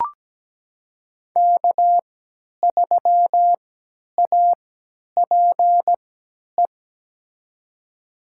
Morse code, heard as a keyed tone, about 17 wpm, spaced as for 10 wpm, 700 Hz. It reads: K3APE
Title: USA Call Signs - 3x Character Spacing - 17wpm